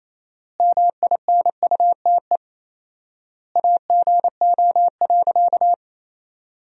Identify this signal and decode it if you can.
Morse code, heard as a keyed tone, about 28 wpm, 700 Hz. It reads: MINUTE AGO.
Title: 4x Word-Spacing - Sentences from Top 500 Words - 28wpm